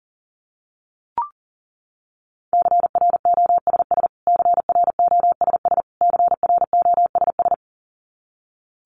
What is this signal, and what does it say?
Morse code, heard as a keyed tone, about 40 wpm, 700 Hz. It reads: CROSS CROSS CROSS
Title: Top 500 Words Repeated 40wpm